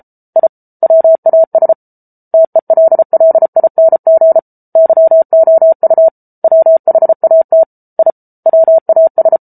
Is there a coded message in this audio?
Yes: Morse code from 0.4 to 9.4 s, heard as a keyed tone, about 34 wpm, 650 Hz. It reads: I WAS TELLING YOU WHAT I WAS